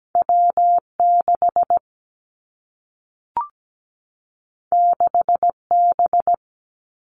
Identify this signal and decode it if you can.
Morse code, heard as a keyed tone, about 17 wpm, 700 Hz. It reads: W6 6B
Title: Two Challenging Letter-Number Combinations 17wpm